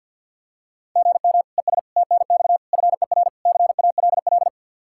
Morse code, heard as a keyed tone, about 48 wpm, 700 Hz. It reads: GM ES TNX FER CALL
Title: Practice QSO - Exchange 1 - Beginning - 50wpm